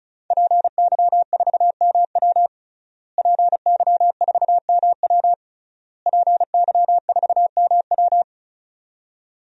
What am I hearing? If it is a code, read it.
Morse code, heard as a keyed tone, about 36 wpm, 700 Hz. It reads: PY4MW PY4MW PY4MW
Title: DXCC-100 Call Signs - 5 Characters - Repeated 3x - 35wpm